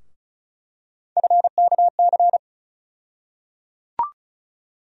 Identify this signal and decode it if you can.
Morse code, heard as a keyed tone, about 36 wpm, 700 Hz. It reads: FKC
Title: USA Call Sign Postfixes - 3 Letters 35wpm